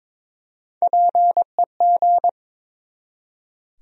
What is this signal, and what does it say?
Morse code, heard as a keyed tone, about 22 wpm, 700 Hz. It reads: PEG
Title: USA Common Names - Max 3 Letters - 22wpm